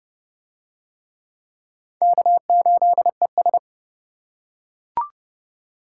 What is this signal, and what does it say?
Morse code, heard as a keyed tone, about 25 wpm, 700 Hz. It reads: K8ES